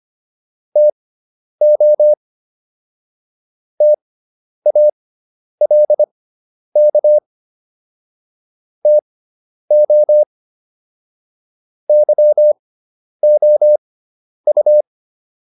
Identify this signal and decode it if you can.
Morse code, heard as a keyed tone, about 25 wpm, spaced as for 10 wpm, 600 Hz. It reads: TO TALK TO YOU